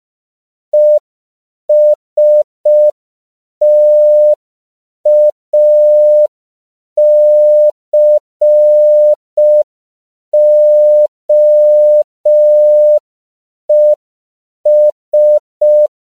Morse code, heard as a keyed tone, about 5 wpm, 600 Hz. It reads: ESTACOES
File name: CW / 5PPM